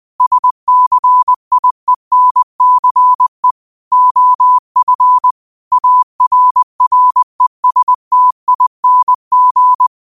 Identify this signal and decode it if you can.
Morse code, heard as a keyed tone, about 20 wpm, 1000 Hz. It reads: SCIENCE OF ARRESTING